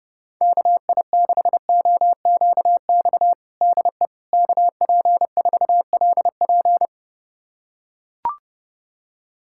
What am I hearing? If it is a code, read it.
Morse code, heard as a keyed tone, about 30 wpm, 700 Hz. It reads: KI6OQX DE KP4LP
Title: USA Callsign DE Callsign 30wpm